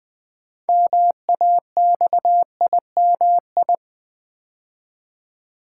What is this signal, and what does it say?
Morse code, heard as a keyed tone, about 21 wpm, 700 Hz. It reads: MAXIMI
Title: Build A Word - 2 Letters At A Time - 8-Letter Words - 20wpm